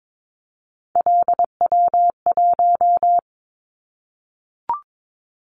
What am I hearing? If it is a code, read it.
Morse code, heard as a keyed tone, about 22 wpm, 700 Hz. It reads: LW1
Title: Callsign Prefixes for Top 50 DXCC - 22wpm